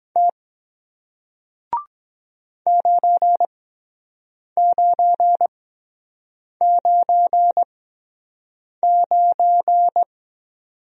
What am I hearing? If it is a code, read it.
Morse code, heard as a keyed tone, about 23 wpm, spaced as for 7 wpm, 700 Hz. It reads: T 9999